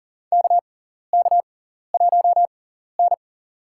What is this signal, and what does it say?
Morse code, heard as a keyed tone, about 39 wpm, spaced as for 14 wpm, 690 Hz. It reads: KK1N
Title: USA Call Signs - 6x Character Spacing - 40wpm